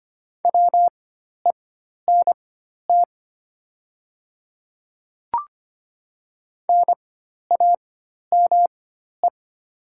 Morse code, heard as a keyed tone, about 25 wpm, spaced as for 12 wpm, 700 Hz. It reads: WENT NAME
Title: 4x Character Spacing - Top 300 Words - 25wpm